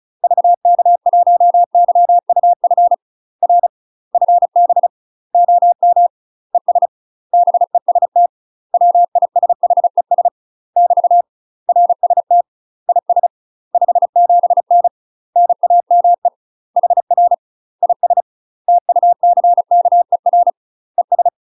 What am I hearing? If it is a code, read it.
Morse code, heard as a keyed tone, about 36 wpm, 700 Hz. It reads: VK1YUF R FB OM ES BEST WISHES <BT> RST IS 57N NAME HR IS TUCKER ES
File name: UK MASTER QSOs TRAINING / 35WPM / UK MASTER QSOs 0012